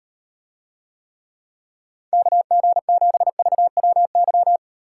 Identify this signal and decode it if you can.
Morse code, heard as a keyed tone, about 38 wpm, 700 Hz. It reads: KG7VWY